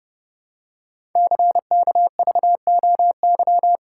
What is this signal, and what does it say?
Morse code, heard as a keyed tone, about 29 wpm, 700 Hz. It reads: CKVOY
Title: Code Groups - 5 Letters - 30wpm